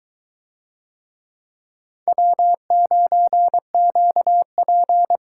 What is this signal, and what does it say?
Morse code, heard as a keyed tone, about 22 wpm, 700 Hz. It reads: W9QP